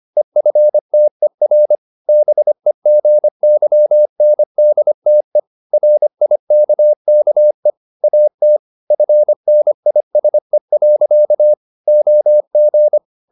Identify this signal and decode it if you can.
Morse code, heard as a keyed tone, about 25 wpm, 600 Hz. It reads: EFTER BEGYNDTE RIKKE AT FNISE. OG
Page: Ferie 25 wpm